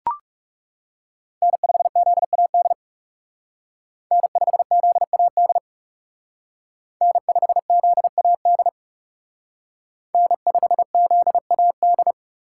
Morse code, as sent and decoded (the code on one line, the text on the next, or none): -. ..... --.. .- -.. / -. ..... --.. .- -.. / -. ..... --.. .- -.. / -. ..... --.. .- -..
N5ZAD N5ZAD N5ZAD N5ZAD